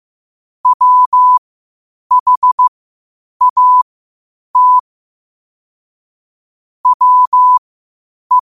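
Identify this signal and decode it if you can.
Morse code, heard as a keyed tone, about 15 wpm, spaced as for 8 wpm, 1000 Hz. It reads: WHAT WE
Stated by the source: Morse code quotes at 8 WPM.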